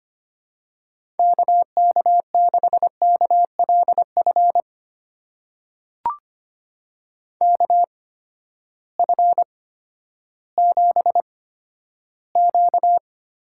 Morse code, heard as a keyed tone, about 25 wpm, spaced as for 12 wpm, 700 Hz. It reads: KK6KLF KF7Q
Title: USA Call Signs - 6 Characters - 8x Character Spacing - 25wpm